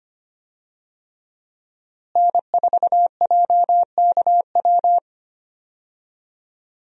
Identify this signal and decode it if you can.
Morse code, heard as a keyed tone, about 25 wpm, 700 Hz. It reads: N4JKW
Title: USA Call Signs - 5 Characters - Repeated 3x - 25wpm